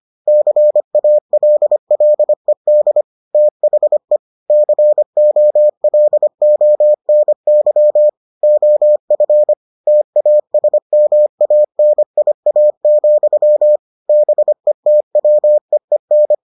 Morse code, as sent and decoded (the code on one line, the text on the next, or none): -.-. .- .-.. .-.. -.. / - .... / -.-. --- .-.. --- -. -.-- / --- ..-. / - .- ... -- .- -. .. .- --..-- / -... - .-- . -.
CALLED THE COLONY OF TASMANIA, BETWEEN